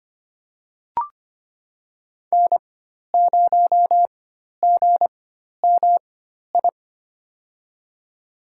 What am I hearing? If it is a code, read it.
Morse code, heard as a keyed tone, about 25 wpm, spaced as for 12 wpm, 700 Hz. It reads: N0GMI
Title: USA Call Signs - 5 Characters - 4x Character Spacing - 25wpm